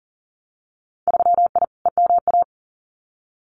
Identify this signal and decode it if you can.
Morse code, heard as a keyed tone, about 42 wpm, 700 Hz. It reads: 3I EMA
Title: Winter Field Day Exchanges Encore 40wpm